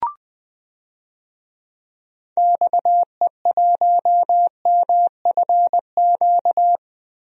Morse code, heard as a keyed tone, about 20 wpm, 700 Hz. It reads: XE1MFQ
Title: DXCC-100 Call Signs - 6 Characters - Repeated 2x - 20wpm